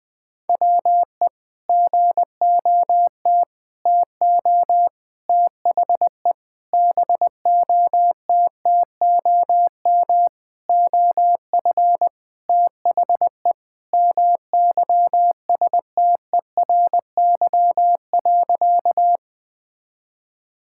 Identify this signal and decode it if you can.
Morse code, heard as a keyed tone, about 20 wpm, 700 Hz. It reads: WE GOT TO THE BOTTOM OF THE MYSTERY.
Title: Sentences with 5-Word Phrases - Spoken Beginning - 20wpm